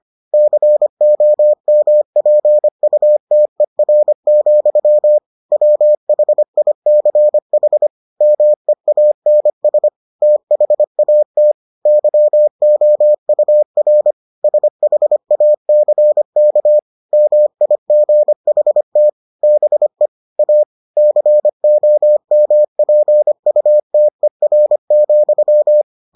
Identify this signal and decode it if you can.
Morse code, heard as a keyed tone, about 25 wpm, 600 Hz. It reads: COMPUTER, WHICH MEANS THAT YOUR SHACK MIGHT BE A COMPUTER,